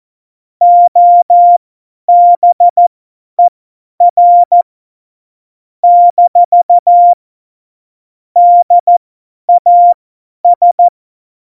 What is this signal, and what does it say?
Morse code, heard as a keyed tone, about 14 wpm, spaced as for 10 wpm, 700 Hz. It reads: OBER - DAS